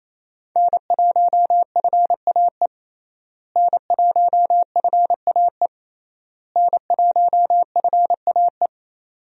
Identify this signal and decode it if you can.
Morse code, heard as a keyed tone, about 28 wpm, 700 Hz. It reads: N1FAE N1FAE N1FAE
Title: USA Call Signs - 5 Characters - Repeated 3x - 28wpm